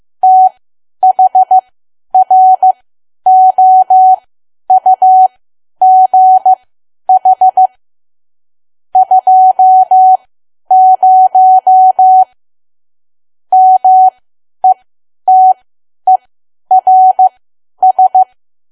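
Morse code, heard as a keyed tone, about 15 wpm, spaced as for 10 wpm, 750 Hz. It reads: THROUGH 20 METERS
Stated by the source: Morse learning practice at 10 wpm.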